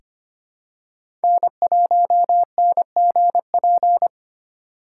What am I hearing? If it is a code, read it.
Morse code, heard as a keyed tone, about 25 wpm, 700 Hz. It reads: N1NGP
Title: USA Call Signs - 5 Characters - 4x Character Spacing - 25wpm